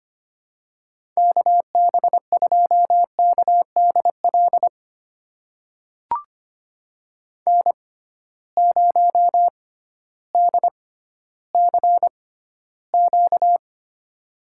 Morse code, heard as a keyed tone, about 25 wpm, spaced as for 13 wpm, 700 Hz. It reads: KB2KDL N0DCQ
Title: USA Call Signs - 6x Character Spacing - 25wpm